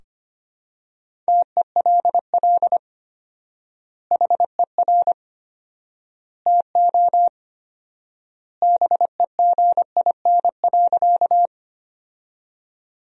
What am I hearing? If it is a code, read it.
Morse code, heard as a keyed tone, about 25 wpm, 700 Hz. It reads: TELL HER TO BEGIN.
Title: 4x Word-Spacing - Sentences from Top 300 Words - 25wpm